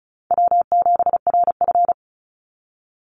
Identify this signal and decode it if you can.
Morse code, heard as a keyed tone, about 34 wpm, 700 Hz. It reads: W7RF
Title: Easy Call Signs Repeated 2x - 35wpm